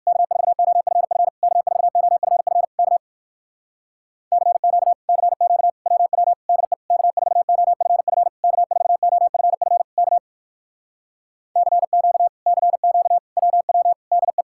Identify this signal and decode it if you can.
Morse code, heard as a keyed tone, about 61 wpm, 700 Hz. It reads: K4OFF K4OFF K CQ CQ WW DE K4OFF K4OFF K CQ CQ WW DE